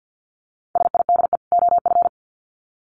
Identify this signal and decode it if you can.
Morse code, heard as a keyed tone, about 50 wpm, 700 Hz. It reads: SIDE OF
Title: Sets of 2 Words 50wpm